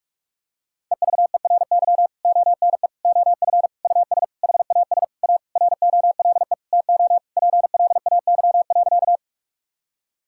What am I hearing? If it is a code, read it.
Morse code, heard as a keyed tone, about 46 wpm, 700 Hz. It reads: EVERY ONE OF US HAS A ROLE TO PLAY.
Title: Sentences from Top 1000 Words - Review Part 6 - 45wpm